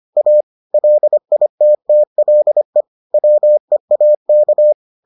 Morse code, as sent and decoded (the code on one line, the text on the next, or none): .- / .-.. .. - - .-.. / .-- . .- -.-
A LITTLE WEAK